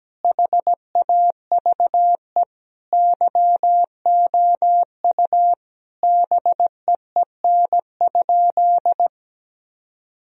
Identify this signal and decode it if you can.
Morse code, heard as a keyed tone, about 17 wpm, 700 Hz. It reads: HAVE YOU BEEN?
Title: Sentences from Top 1000 Words - Review Part 3 - 17wpm